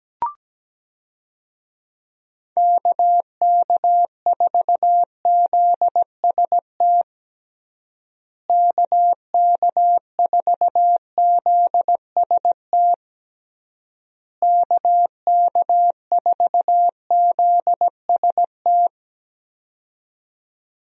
Morse code, heard as a keyed tone, about 17 wpm, 700 Hz. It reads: KK4ZST KK4ZST KK4ZST
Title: USA Call Signs - 6 Characters - Repeated 3x - 17wpm